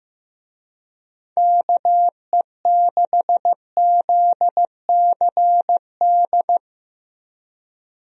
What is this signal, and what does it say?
Morse code, heard as a keyed tone, about 15 wpm, 700 Hz. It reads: KE6ZCD
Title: USA Call Signs - 6 Characters - Repeated 2x - 15wpm